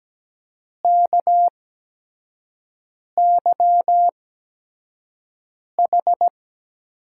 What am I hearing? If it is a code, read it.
Morse code, heard as a keyed tone, about 17 wpm, spaced as for 5 wpm, 700 Hz. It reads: KYH
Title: USA Call Signs - 8x Character Spacing - 17wpm